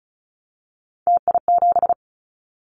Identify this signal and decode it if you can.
Morse code, heard as a keyed tone, about 36 wpm, 700 Hz. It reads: TI7
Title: Callsign Prefixes for Top 100 DXCC - 35wpm